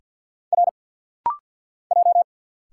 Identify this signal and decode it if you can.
Morse code, heard as a keyed tone, about 50 wpm, 700 Hz. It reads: R J
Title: Single Letters - Rapid-Fire 50wpm